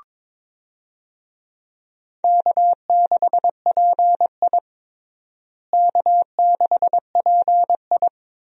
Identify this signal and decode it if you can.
Morse code, heard as a keyed tone, about 21 wpm, 700 Hz. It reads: K6PI K6PI
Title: USA Call Signs - 4 Characters - Repeated 2x - 22wpm